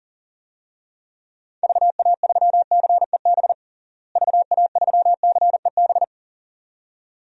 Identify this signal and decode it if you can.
Morse code, heard as a keyed tone, about 40 wpm, 700 Hz. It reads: VA3CEB VA3CEB
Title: DXCC-100 Call Signs - 6 Characters - Repeated 2x - 40wpm